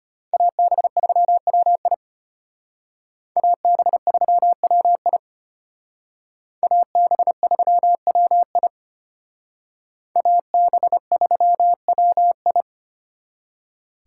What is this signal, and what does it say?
Morse code, heard as a keyed tone, about 31 wpm, 700 Hz. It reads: AB3WI AB3WI AB3WI AB3WI
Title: Speed Racing - USA 2x2 Call Signs 25wpm